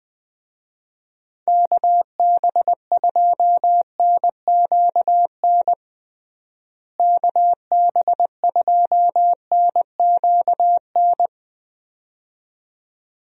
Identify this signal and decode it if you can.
Morse code, heard as a keyed tone, about 20 wpm, 700 Hz. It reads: KB2NQN KB2NQN
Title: USA Call Signs - 6 Characters - Repeated 2x - 20wpm